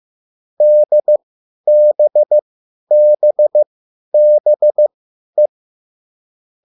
Morse code, heard as a keyed tone, about 15 wpm, spaced as for 10 wpm, 600 Hz. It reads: DBBBE
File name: QSL / CW lesson